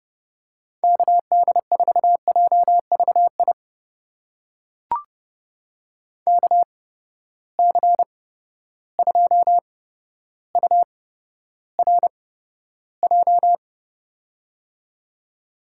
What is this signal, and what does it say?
Morse code, heard as a keyed tone, about 30 wpm, spaced as for 13 wpm, 700 Hz. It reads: KD4JVI KC2URJ
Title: USA Call Signs - 6 Characters - 8x Character Spacing - 30wpm